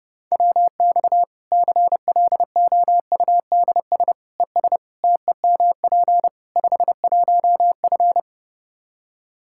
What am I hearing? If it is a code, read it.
Morse code, heard as a keyed tone, about 30 wpm, 700 Hz. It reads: WX CLOUDS ES TEMP 51F
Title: Practice QSO - Exchange 2 - Info - 30wpm